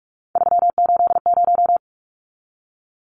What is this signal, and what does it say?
Morse code, heard as a keyed tone, about 44 wpm, 700 Hz. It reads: 380
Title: Numbers - 3-Digit 45wpm